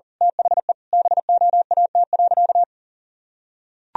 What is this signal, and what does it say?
Morse code, heard as a keyed tone, about 40 wpm, 700 Hz. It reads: THE BOAT.
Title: Sentences from Top 1000 Words - Review Part 3 - 40wpm